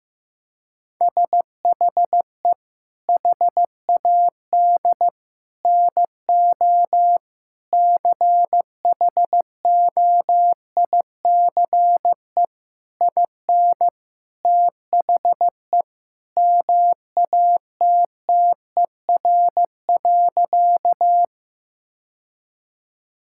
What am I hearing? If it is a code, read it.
Morse code, heard as a keyed tone, about 15 wpm, 700 Hz. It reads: SHE HAD NO CHOICE IN THE MATTER.
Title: Sentences from Top 1000 Words - Review Part 6 - 15wpm